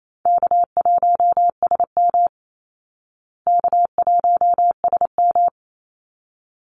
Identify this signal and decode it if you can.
Morse code, heard as a keyed tone, about 28 wpm, 700 Hz. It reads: K1SM K1SM
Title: Easy Call Signs Repeated 2x - 28wpm